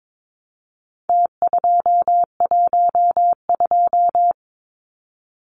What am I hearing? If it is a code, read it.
Morse code, heard as a keyed tone, about 22 wpm, 700 Hz. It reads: T212
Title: Numbers - Contest Style - 4-Digit 22wpm